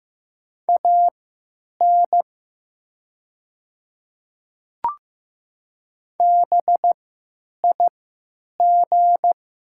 Morse code, heard as a keyed tone, about 15 wpm, spaced as for 8 wpm, 700 Hz. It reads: AN BIG
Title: 3x Character Spacing - Top 200 Words - 15wpm